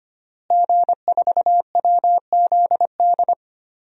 Morse code, heard as a keyed tone, about 25 wpm, 700 Hz. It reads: G4WZD
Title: DXCC-100 Call Signs - 5 Characters - Repeated 2x - 25wpm